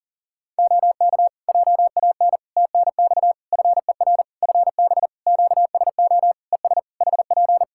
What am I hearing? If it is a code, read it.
Morse code, heard as a keyed tone, about 40 wpm, 700 Hz. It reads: OK JAN TNX FER FB QSO ES HP
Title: K1USN SST QSO Practice 40wpm